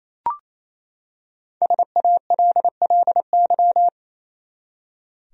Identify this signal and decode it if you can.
Morse code, heard as a keyed tone, about 28 wpm, 700 Hz. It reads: SALLY
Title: USA Common Names - Max 5 Letters - 28wpm